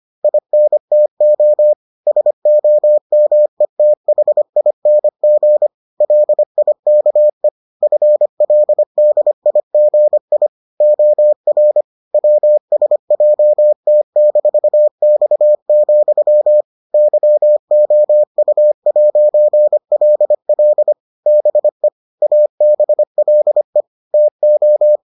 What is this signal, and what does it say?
Morse code, heard as a keyed tone, about 25 wpm, 600 Hz. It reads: INTO SOMETHING LIKE FLDIGI OR WSJT-X, YOU'LL BE ABLE TO